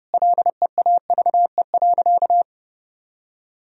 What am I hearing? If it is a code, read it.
Morse code, heard as a keyed tone, about 30 wpm, 700 Hz. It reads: LEAVE.
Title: Sentences from Top 1000 Words - Review Part 2 - 30wpm